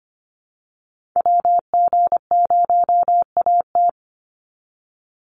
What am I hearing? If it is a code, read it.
Morse code, heard as a keyed tone, about 25 wpm, 700 Hz. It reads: WG0AT
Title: NA SOTA Chaser Callsigns 25wpm